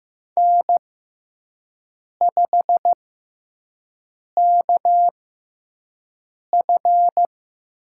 Morse code, heard as a keyed tone, about 15 wpm, spaced as for 5 wpm, 700 Hz. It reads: N5KF
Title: USA Call Signs - 4 Characters - 6x Character Spacing - 15wpm